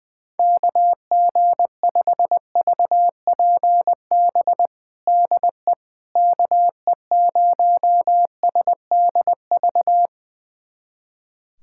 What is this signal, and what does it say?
Morse code, heard as a keyed tone, about 20 wpm, 700 Hz. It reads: KG5VPB DE KE0SDV
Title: USA Callsign DE Callsign 20wpm